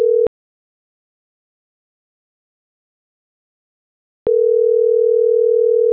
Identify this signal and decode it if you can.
Ringback tone (North America), 440 + 480 Hz.
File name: CENTRAL MORENA / out jacque.aguilera.25